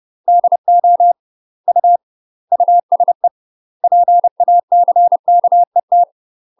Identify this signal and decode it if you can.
Morse code, heard as a keyed tone, about 30 wpm, 700 Hz. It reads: DO U USE PACKET
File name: UK MASTER QSOs TRAINING / 30WPM / UK MASTER QSOs 0038